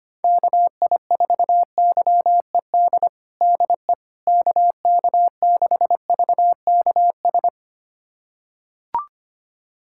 Morse code, heard as a keyed tone, about 25 wpm, 700 Hz. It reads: KI4YED DE KK6VKS
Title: USA Callsign DE Callsign - 6 Characters - 25wpm